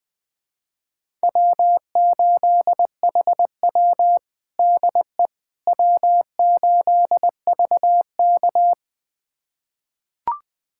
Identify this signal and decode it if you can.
Morse code, heard as a keyed tone, about 21 wpm, 700 Hz. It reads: W8HW DE W8VK
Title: USA Callsign DE Callsign - 4 Characters - 20wpm